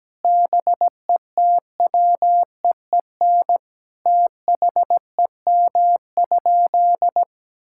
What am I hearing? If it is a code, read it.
Morse code, heard as a keyed tone, about 17 wpm, 700 Hz. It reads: BETWEEN THEM?
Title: Sentences from Top 1000 Words - Review Part 6 - 17wpm